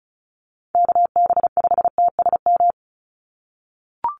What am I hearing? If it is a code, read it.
Morse code, heard as a keyed tone, about 34 wpm, 700 Hz. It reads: KB5TSM
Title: USA Call Signs 35wpm